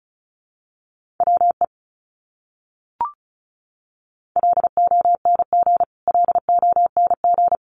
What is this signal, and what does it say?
Morse code, heard as a keyed tone, about 34 wpm, 700 Hz. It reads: WE LONG LONG
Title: Top 200 Words Repeated 35wpm